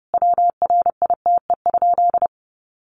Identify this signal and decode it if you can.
Morse code, heard as a keyed tone, about 28 wpm, 700 Hz. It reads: WRITE?
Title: Sentences from Top 600 Words 30wpm